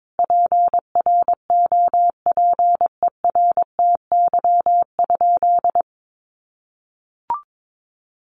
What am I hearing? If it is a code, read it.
Morse code, heard as a keyed tone, about 22 wpm, 700 Hz. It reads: PROPERTY?
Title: Sentences from Top 1500 Words 22wpm